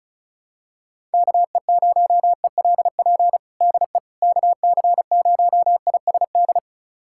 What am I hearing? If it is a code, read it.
Morse code, heard as a keyed tone, about 36 wpm, 700 Hz. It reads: KE0ELP DE KC0ISD
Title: USA Callsign DE Callsign - 6 Characters - 35wpm